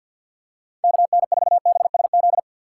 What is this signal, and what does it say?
Morse code, heard as a keyed tone, about 53 wpm, 700 Hz. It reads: KN4BSZ
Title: USA Call Signs - 6 Characters - 6x Character Spacing - 50wpm